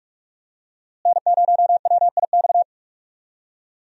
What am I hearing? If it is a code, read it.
Morse code, heard as a keyed tone, about 45 wpm, 700 Hz. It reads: N0WIX
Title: USA Call Signs - 5 Characters - Repeated 3x - 45wpm